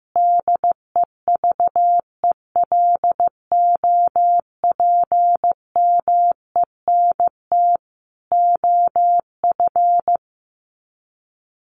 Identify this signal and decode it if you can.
Morse code, heard as a keyed tone, about 15 wpm, 700 Hz. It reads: DEVELOPMENT OF
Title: Sets of 3 Words 15wpm